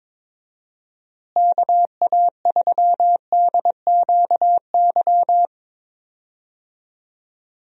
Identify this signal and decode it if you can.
Morse code, heard as a keyed tone, about 22 wpm, 700 Hz. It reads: KA3DQY
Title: USA Call Signs - 6 Characters - Repeated 3x - 22wpm